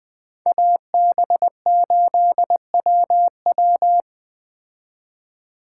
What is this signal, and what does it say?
Morse code, heard as a keyed tone, about 20 wpm, 700 Hz. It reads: AB8WW